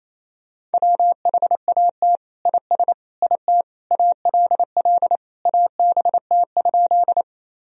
Morse code, heard as a keyed tone, about 27 wpm, 700 Hz. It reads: WHAT IS IT ALL ABT?
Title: Sentences from Top 100 Words and Abbreviations 28wpm